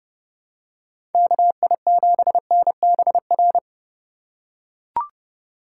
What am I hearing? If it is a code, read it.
Morse code, heard as a keyed tone, about 30 wpm, 700 Hz. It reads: KI7NBR
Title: USA Call Signs - 6 Characters - 3x Character Spacing - 30wpm